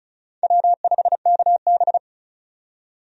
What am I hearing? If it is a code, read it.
Morse code, heard as a keyed tone, about 36 wpm, 700 Hz. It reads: W5KB
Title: USA Call Signs - 4 Characters - 35wpm